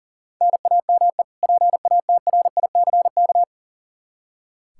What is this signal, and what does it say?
Morse code, heard as a keyed tone, about 40 wpm, 700 Hz. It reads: NAME PATRICK
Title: K1USN SST QSO Practice 40wpm